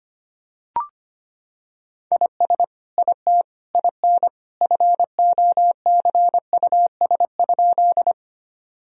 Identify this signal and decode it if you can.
Morse code, heard as a keyed tone, about 25 wpm, 700 Hz. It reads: IS IT IN FOCUS?
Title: Sentences from Top 1000 Words - Review Part 4 - 25wpm